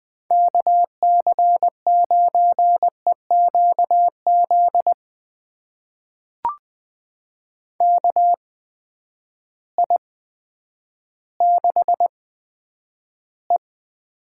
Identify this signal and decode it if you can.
Morse code, heard as a keyed tone, about 20 wpm, spaced as for 10 wpm, 700 Hz. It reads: KC9EQZ KI6E
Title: USA Call Signs - 6 Characters - 8x Character Spacing - 20wpm